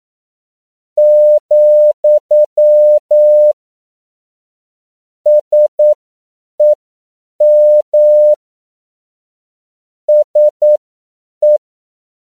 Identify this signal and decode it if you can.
Morse code, heard as a keyed tone, about 9 wpm, spaced as for 7 wpm, 600 Hz. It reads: , SEM SE